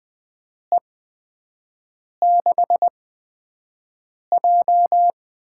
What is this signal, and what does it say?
Morse code, heard as a keyed tone, about 20 wpm, spaced as for 5 wpm, 700 Hz. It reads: E6J
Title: USA Call Signs - 6 Characters - 8x Character Spacing - 20wpm